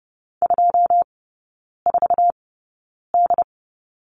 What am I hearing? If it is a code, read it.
Morse code, heard as a keyed tone, about 30 wpm, spaced as for 9 wpm, 700 Hz. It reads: 24D
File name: Lesson 17e - Letter u - Three Characters 30wpm